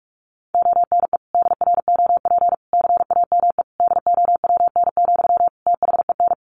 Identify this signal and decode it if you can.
Morse code, heard as a keyed tone, about 44 wpm, 700 Hz. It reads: ONE DROP CAME DOWN, THEN
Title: Sentences from Top 1400 Words 45wpm